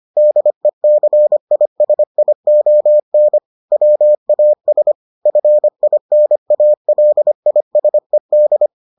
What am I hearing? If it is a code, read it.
Morse code, heard as a keyed tone, about 25 wpm, 600 Hz. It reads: DECISION WAS FINALISED